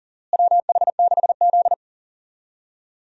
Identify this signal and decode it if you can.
Morse code, heard as a keyed tone, about 40 wpm, 700 Hz. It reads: WH6Z
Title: USA Call Signs - 4 Characters - Repeated 2x - 40wpm